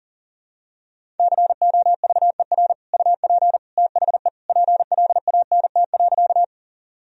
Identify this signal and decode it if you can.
Morse code, heard as a keyed tone, about 40 wpm, 700 Hz. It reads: COVER UP THE PLANT.
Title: Sentences from Top 1000 Words - Review Part 1 - 40wpm